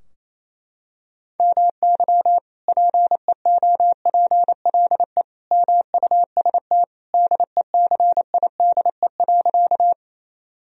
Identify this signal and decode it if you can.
Morse code, heard as a keyed tone, about 28 wpm, 700 Hz. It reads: MY PEOPLE MUST DECIDE.
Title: Sentences from Top 1000 Words - Review Part 3 - 28wpm